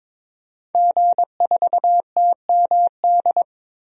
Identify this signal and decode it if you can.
Morse code, heard as a keyed tone, about 22 wpm, 700 Hz. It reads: G4TMD
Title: DXCC-100 Call Signs - 5 Characters - 22wpm